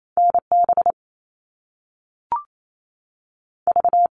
Morse code, heard as a keyed tone, about 28 wpm, 700 Hz. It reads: NB V
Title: All Cut Numbers - 2-Digit 28wpm